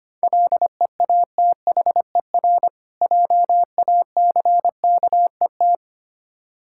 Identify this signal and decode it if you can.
Morse code, heard as a keyed tone, about 25 wpm, 700 Hz. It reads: LEATHER JACKET
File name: practice / 25wpm / Difficult Words with Hints - 25wpm